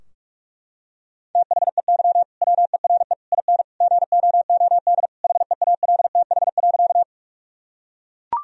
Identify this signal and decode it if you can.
Morse code, heard as a keyed tone, about 42 wpm, 700 Hz. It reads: THEY WERE IN GOOD HEALTH.
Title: Sentences from Top 1000 Words - Review Part 3 - 45wpm